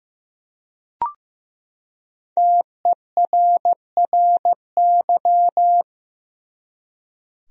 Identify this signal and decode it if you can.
Morse code, heard as a keyed tone, about 15 wpm, 700 Hz. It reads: TERRY
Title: USA Common Names - 15wpm